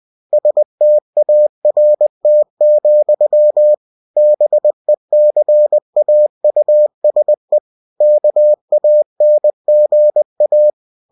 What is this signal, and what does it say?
Morse code, heard as a keyed tone, about 20 wpm, 600 Hz. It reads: START, BECAUSE KANGA
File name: cw-books / winnie-the-pooh / winnie-20wpm-page-0063